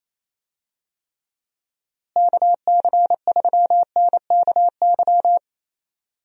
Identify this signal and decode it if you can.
Morse code, heard as a keyed tone, about 28 wpm, 700 Hz. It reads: KC3NKY
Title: USA Call Signs - 6 Characters - Repeated 3x - 28wpm